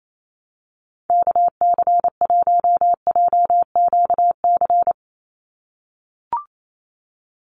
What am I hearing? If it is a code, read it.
Morse code, heard as a keyed tone, about 28 wpm, 700 Hz. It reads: KC1JQC
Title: USA Call Signs 28wpm